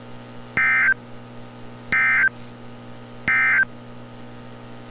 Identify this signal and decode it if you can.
Emergency Alert System (SAME) tones, heard as harsh buzzing bursts. It is heard from 0.5 to 3.7 s.
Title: NOAA Weather Radio - Altoona, PA